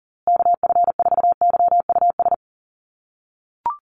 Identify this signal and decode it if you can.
Morse code, heard as a keyed tone, about 40 wpm, 700 Hz. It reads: KF4YUS
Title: USA Call Signs Repeated 3x 40wpm